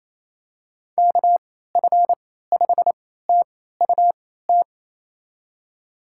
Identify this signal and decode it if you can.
Morse code, heard as a keyed tone, about 28 wpm, spaced as for 16 wpm, 700 Hz. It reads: KF5TUT
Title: USA Call Signs - 6 Characters - 3x Character Spacing - 28wpm